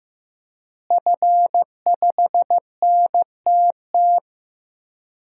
Morse code, heard as a keyed tone, about 15 wpm, 700 Hz. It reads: F5NTT
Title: DXCC-100 Call Signs - 5 Characters - Repeated 2x - 15wpm